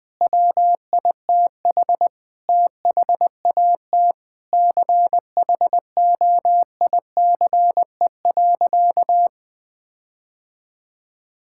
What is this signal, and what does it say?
Morse code, heard as a keyed tone, about 20 wpm, 700 Hz. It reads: WITH THAT CHOICE.
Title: Sentences with 7-Word Phrases - 20wpm